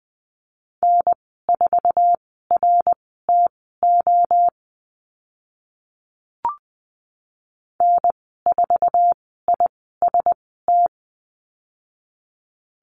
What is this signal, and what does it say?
Morse code, heard as a keyed tone, about 20 wpm, spaced as for 15 wpm, 700 Hz. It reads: N4RTO N4IST
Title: 088 Lesson 15g - Number 5 - Call Signs 20wpm